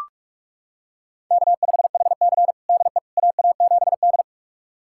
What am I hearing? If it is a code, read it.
Morse code, heard as a keyed tone, about 41 wpm, 700 Hz. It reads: K5HC DE AA7D
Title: USA Callsign DE Callsign - 4 Characters - 45wpm